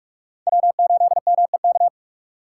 Morse code, heard as a keyed tone, about 56 wpm, 700 Hz. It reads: W8MEK
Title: Speed Racing - USA 1x3 Call Signs 45wpm